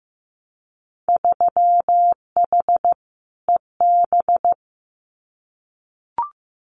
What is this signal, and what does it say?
Morse code, heard as a keyed tone, about 15 wpm, 700 Hz. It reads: 3H EB
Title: Winter Field Day Exchanges Encore 15wpm